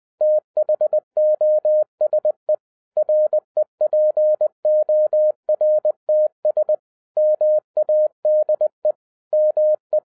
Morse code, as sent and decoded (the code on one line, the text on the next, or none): - .... --- ... / .-. .--. --- .-. - ... / -- .- -.. / -- .
THOSE REPORTS MADE ME